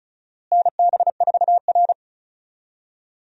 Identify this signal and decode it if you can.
Morse code, heard as a keyed tone, about 36 wpm, 700 Hz. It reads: NB4R
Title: USA Call Signs - 4 Characters - Repeated 2x - 35wpm